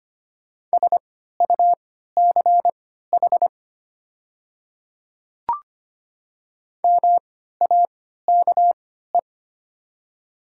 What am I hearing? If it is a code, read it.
Morse code, heard as a keyed tone, about 25 wpm, spaced as for 14 wpm, 690 Hz. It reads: SUCH MAKE
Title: 3x Character Spacing - Top 500 Words - 25wpm